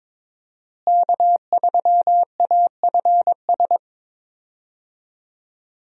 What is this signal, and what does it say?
Morse code, heard as a keyed tone, about 22 wpm, 700 Hz. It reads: K3AFS
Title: USA Call Signs - 5 Characters - Repeated 2x - 22wpm